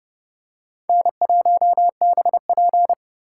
Morse code, heard as a keyed tone, about 30 wpm, 700 Hz. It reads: N1BP